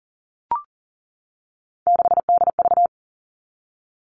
Morse code, heard as a keyed tone, about 40 wpm, 700 Hz. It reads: 6DV
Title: All Cut Numbers - 3-Digit 40wpm